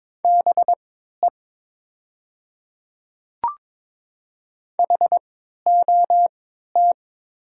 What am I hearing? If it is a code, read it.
Morse code, heard as a keyed tone, about 22 wpm, spaced as for 13 wpm, 700 Hz. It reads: BE HOT
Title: 3x Character Spacing - Top 100 Words - 22wpm